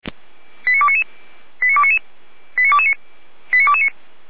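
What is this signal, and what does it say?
Five-tone selective call (ZVEI), heard as quick runs of five tones, tones of about 70 ms. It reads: 89299 89299 89229 89229
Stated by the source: ZVEI five-tone call received on an SDR, from the feuerwehr-orlamuende.org fire brigade audio archive.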